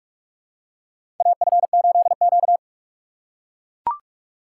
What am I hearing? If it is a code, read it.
Morse code, heard as a keyed tone, about 44 wpm, 700 Hz. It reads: AF8Q
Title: USA Call Signs - 4 Characters - 3x Character Spacing - 45wpm